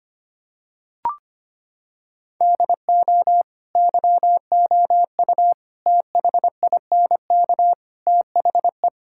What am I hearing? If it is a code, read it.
Morse code, heard as a keyed tone, about 25 wpm, 700 Hz. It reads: DO YOU THINK THE
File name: practice / 25wpm / Sentences with 4-Word Phrases - 25wpm